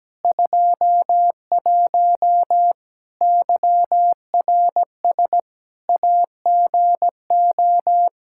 Morse code, heard as a keyed tone, about 17 wpm, 700 Hz. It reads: 21 YRS AGO
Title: Practice QSO - Exchange 3 - More Info - 17wpm